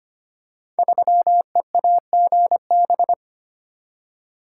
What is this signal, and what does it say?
Morse code, heard as a keyed tone, about 25 wpm, 700 Hz. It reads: 3EAGB
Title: Code Groups - 5 Letter:Numbers - 25wpm